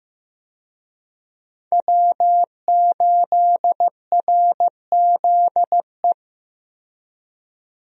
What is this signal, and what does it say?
Morse code, heard as a keyed tone, about 15 wpm, 700 Hz. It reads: W8RZE
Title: USA Call Signs - 5 Characters - 15wpm